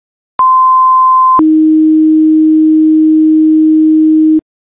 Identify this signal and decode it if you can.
Two-tone page, heard as a one-second tone, then a three-second tone, about 1040 then 310 Hz. It is heard from 0.4 to 4.4 s.